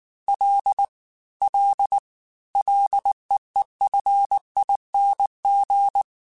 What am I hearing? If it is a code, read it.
Morse code, heard as a keyed tone, about 19 wpm, 800 Hz. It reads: L L LEEFING
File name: Ls / Single L m3s / Single L @18